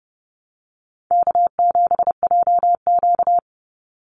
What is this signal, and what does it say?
Morse code, heard as a keyed tone, about 30 wpm, 700 Hz. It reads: K7JQ
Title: Easy Call Signs Repeated 2x - 30wpm